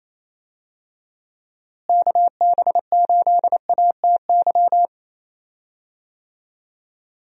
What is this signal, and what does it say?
Morse code, heard as a keyed tone, about 28 wpm, 700 Hz. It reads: KB8ATY
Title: USA Call Signs - 6 Characters - Repeated 2x - 28wpm